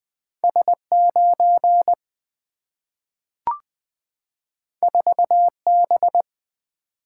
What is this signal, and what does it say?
Morse code, heard as a keyed tone, about 20 wpm, 700 Hz. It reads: S9 4B
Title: Single-Letter Single-Number 20wpm